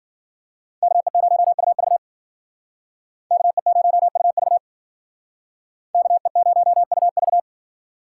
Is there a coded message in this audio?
Yes: Morse code from 0.8 to 7.4 s, heard as a keyed tone, about 53 wpm, 700 Hz. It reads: KE0UV KE0UV KE0UV